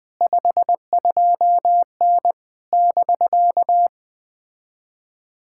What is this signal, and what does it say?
Morse code, heard as a keyed tone, about 20 wpm, 700 Hz. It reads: 52N <BK>
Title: POTA QSO Practice - 20wpm